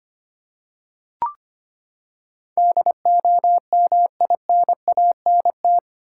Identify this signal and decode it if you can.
Morse code, heard as a keyed tone, about 25 wpm, 700 Hz. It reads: DOMINANT
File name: Eight Letter Words - 25wpm